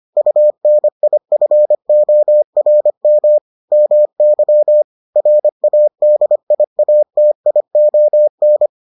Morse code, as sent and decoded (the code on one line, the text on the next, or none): ..- -. .. ..-. --- .-. -- / -- -.-- / .-. .- -.. .. .- - .. --- -.
UNIFORM MY RADIATION